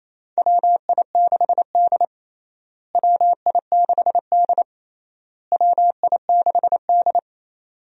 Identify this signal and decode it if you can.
Morse code, heard as a keyed tone, about 28 wpm, 700 Hz. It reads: WI6D WI6D WI6D
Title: USA Call Signs - 4 Characters - Repeated 3x - 28wpm